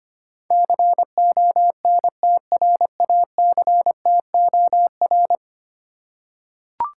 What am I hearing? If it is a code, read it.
Morse code, heard as a keyed tone, about 25 wpm, 700 Hz. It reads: CONTRACTOR
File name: Ten Letter Words - 25wpm